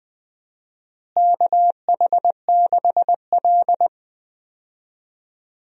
Morse code, heard as a keyed tone, about 20 wpm, 700 Hz. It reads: KH6L
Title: DXCC-100 Call Signs - 4 Characters - Repeated 3x - 20wpm